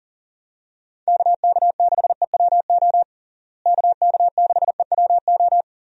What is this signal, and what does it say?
Morse code, heard as a keyed tone, about 37 wpm, 700 Hz. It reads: KK6EWO KK6EWO
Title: USA Call Signs - 6 Characters - Repeated 2x - 40wpm